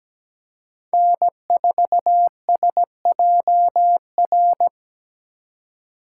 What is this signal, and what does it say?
Morse code, heard as a keyed tone, about 20 wpm, 700 Hz. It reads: N4SJR
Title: Speed Racing - USA Call Signs 17wpm